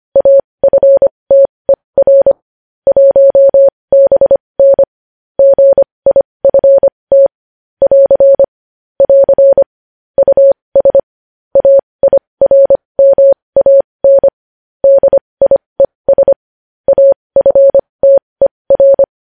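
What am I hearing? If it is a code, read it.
Morse code, heard as a keyed tone, about 25 wpm, spaced as for 20 wpm, 550 Hz. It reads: AFTER 1BN GIFT <AR> <AR> US AIRMAN DIES AFTER
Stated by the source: Morse code at 20 wpm.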